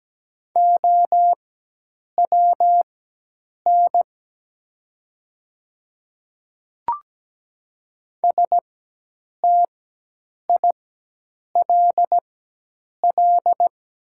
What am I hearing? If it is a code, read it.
Morse code, heard as a keyed tone, about 17 wpm, spaced as for 8 wpm, 700 Hz. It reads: OWN STILL
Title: 4x Character Spacing - Top 300 Words - 17wpm